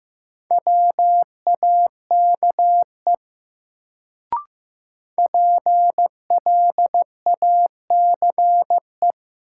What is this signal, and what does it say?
Morse code, heard as a keyed tone, about 15 wpm, 700 Hz. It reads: WAKE PLACE
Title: Sets of 4 Words - New Word Drill - 15wpm